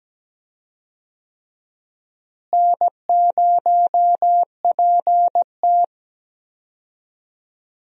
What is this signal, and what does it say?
Morse code, heard as a keyed tone, about 17 wpm, 700 Hz. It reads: N0PT
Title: USA Call Signs - 4 Characters - Repeated 3x - 17wpm